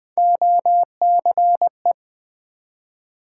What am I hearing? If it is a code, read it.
Morse code, heard as a keyed tone, about 20 wpm, 700 Hz. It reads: OCE
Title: Build A Word - 1 Letter At A Time - 5-Letter Words - 20wpm